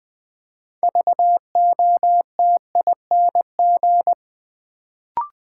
Morse code, heard as a keyed tone, about 20 wpm, 700 Hz. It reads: VOTING
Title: Build A Word - 3 Letters At A Time - 6-Letter Words - 20wpm